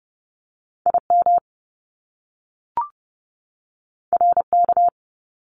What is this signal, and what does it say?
Morse code, heard as a keyed tone, about 30 wpm, 700 Hz. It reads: I M R K
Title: Top Two Letter Combinations in Words 30wpm